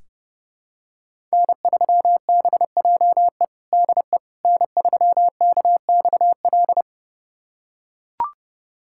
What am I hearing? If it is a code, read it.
Morse code, heard as a keyed tone, about 32 wpm, 700 Hz. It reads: N3BJE DE N3KXL
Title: USA Callsign DE Callsign - 5 Characters - 30wpm